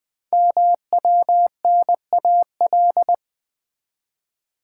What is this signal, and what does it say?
Morse code, heard as a keyed tone, about 20 wpm, 700 Hz. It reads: MWNAL
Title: Code Groups - 5 Letters - 20wpm